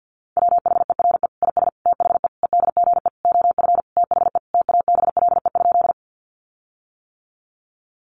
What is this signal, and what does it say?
Morse code, heard as a keyed tone, about 50 wpm, 700 Hz. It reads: WHERE IS THE EDGE OF THE TABLE?
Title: Sentences from Top 1300 Words 50wpm